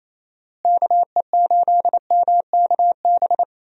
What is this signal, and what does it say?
Morse code, heard as a keyed tone, about 28 wpm, 700 Hz. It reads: KE8MKB
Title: USA Call Signs - 6 Characters - Repeated 2x - 28wpm